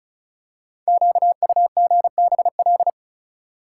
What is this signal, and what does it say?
Morse code, heard as a keyed tone, about 36 wpm, 700 Hz. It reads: QUGBL
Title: Code Groups - 5 Letters - 35wpm